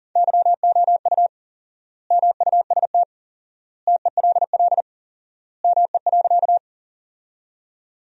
Morse code, heard as a keyed tone, about 37 wpm, 700 Hz. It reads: YOU MUST TELL ME.
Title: 4x Word-Spacing - Sentences from Top 200 Words - 40wpm